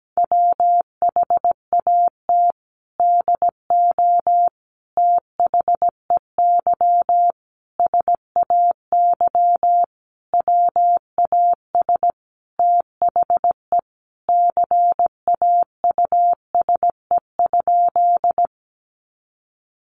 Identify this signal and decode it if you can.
Morse code, heard as a keyed tone, about 17 wpm, 700 Hz. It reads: WHAT DO THEY SAY WAS THE CAUSE?
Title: Sentences from Top 200 Words 17wpm